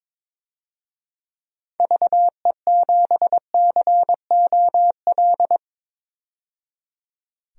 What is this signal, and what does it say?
Morse code, heard as a keyed tone, about 22 wpm, 700 Hz. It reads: VE7COL
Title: DXCC-100 Call Signs - 6 Characters - 22wpm